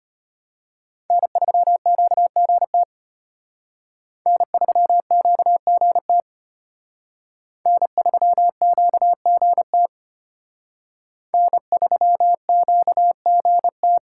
Morse code, heard as a keyed tone, about 32 wpm, 700 Hz. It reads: N3QGT N3QGT N3QGT N3QGT